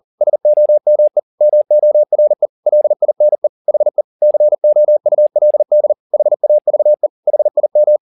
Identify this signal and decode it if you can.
Morse code, heard as a keyed tone, about 40 wpm, 600 Hz. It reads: SOME MORE LINE HE COULD HAVE HIM
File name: the old man and the sea 2 40wpm